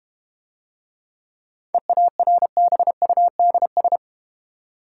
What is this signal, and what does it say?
Morse code, heard as a keyed tone, about 31 wpm, 700 Hz. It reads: EARBUDS
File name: practice / 25wpm / Difficult Words - Speed-Racing with Hints - 25wpm